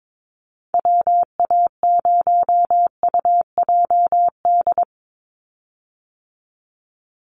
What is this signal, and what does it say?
Morse code, heard as a keyed tone, about 22 wpm, 700 Hz. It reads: WA0UJD
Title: USA Call Signs 22wpm